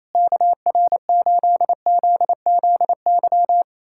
Morse code, heard as a keyed tone, about 28 wpm, 700 Hz. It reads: KR8ZZY
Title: SOTA QSO Practice 28wpm